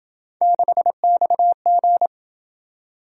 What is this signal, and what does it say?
Morse code, heard as a keyed tone, about 27 wpm, 700 Hz. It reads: 6XG